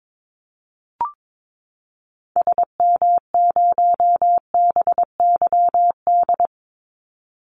Morse code, heard as a keyed tone, about 22 wpm, 700 Hz. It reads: SM0BYD
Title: Call Signs Repeated 2x - 22wpm